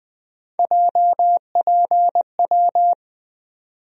Morse code, heard as a keyed tone, about 20 wpm, 700 Hz. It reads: JPW
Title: USA Call Sign Postfixes - 3 Letters 20wpm